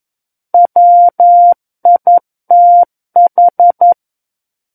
Morse code, heard as a keyed tone, about 11 wpm, 700 Hz. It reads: WITH
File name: 11 WPM